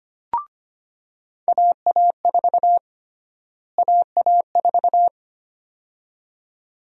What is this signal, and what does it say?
Morse code, heard as a keyed tone, about 25 wpm, 700 Hz. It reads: AA4 AA4
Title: USA Call Sign Prefixes - Repeated 2x - 25wpm